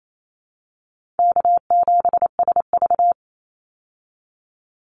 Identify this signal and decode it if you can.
Morse code, heard as a keyed tone, about 28 wpm, 700 Hz. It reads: K7SV
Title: Call Signs - 28wpm